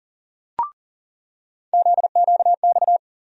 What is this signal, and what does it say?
Morse code, heard as a keyed tone, about 40 wpm, 700 Hz. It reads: ZQX
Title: USA Call Sign Postfixes - 3 Letters 40wpm